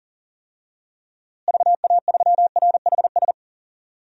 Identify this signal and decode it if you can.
Morse code, heard as a keyed tone, about 40 wpm, 700 Hz. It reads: VA3RHS
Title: DXCC-100 Call Signs - 6 Characters - Repeated 2x - 40wpm